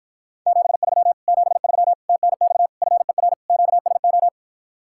Morse code, heard as a keyed tone, about 53 wpm, 700 Hz. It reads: 73 73 TNX FER QSO